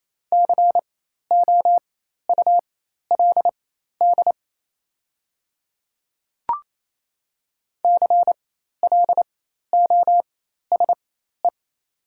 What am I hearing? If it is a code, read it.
Morse code, heard as a keyed tone, about 28 wpm, spaced as for 13 wpm, 700 Hz. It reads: COULD CLOSE
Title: 4x Character Spacing - Top 300 Words - 28wpm